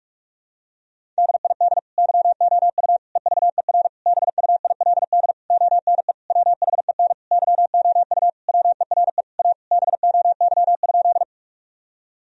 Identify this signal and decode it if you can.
Morse code, heard as a keyed tone, about 46 wpm, 700 Hz. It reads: DID YOU EVER BUILD ONE WHEN YOU WERE A BOY?
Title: Sentences from Top 1000 Words - Review Part 1 - 45wpm